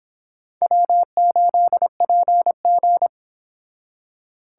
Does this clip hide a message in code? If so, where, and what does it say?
Yes: Morse code from 0.6 to 3.1 s, heard as a keyed tone, about 31 wpm, 700 Hz. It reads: W8PG